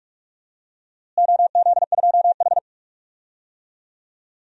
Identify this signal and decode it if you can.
Morse code, heard as a keyed tone, about 46 wpm, 700 Hz. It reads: OZ2H
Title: DXCC-100 Call Signs - 4 Characters - Repeated 3x - 45wpm